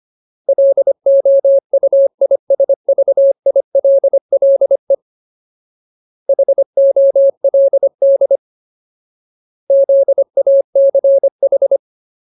Morse code, heard as a keyed tone, about 25 wpm, 550 Hz. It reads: LOUISVILLE HOLD ZACH
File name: Ultimate Word List / 25WPM Ultimate